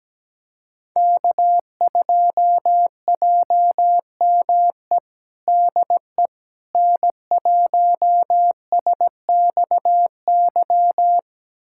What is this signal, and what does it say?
Morse code, heard as a keyed tone, about 17 wpm, 700 Hz. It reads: K2JME DE N1SXY
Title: USA Callsign DE Callsign - 5 Characters - 17wpm